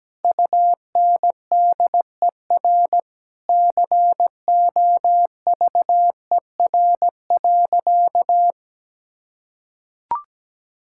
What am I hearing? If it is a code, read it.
Morse code, heard as a keyed tone, about 17 wpm, 700 Hz. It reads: UNDER COVER.
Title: Sentences from Top 1000 Words - Review Part 6 - 17wpm